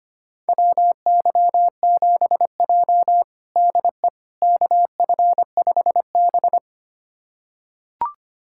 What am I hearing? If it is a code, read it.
Morse code, heard as a keyed tone, about 25 wpm, 700 Hz. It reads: WY7J DE KF5B
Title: USA Callsign DE Callsign - 4 Characters - 25wpm